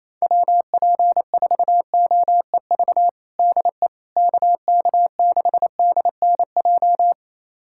Morse code, heard as a keyed tone, about 28 wpm, 700 Hz. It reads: WP4OEV DE KK6DNJ
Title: USA Callsign DE Callsign - 6 Characters - 28wpm